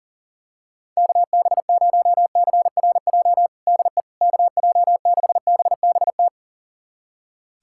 Morse code, heard as a keyed tone, about 40 wpm, 700 Hz. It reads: KB0CRJ DE KJ6BBT
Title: USA Callsign DE Callsign - 6 Characters - 40wpm